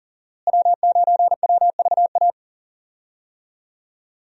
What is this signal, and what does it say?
Morse code, heard as a keyed tone, about 40 wpm, 700 Hz. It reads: W9WVA
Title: USA Call Signs - 5 Characters - Repeated 2x - 40wpm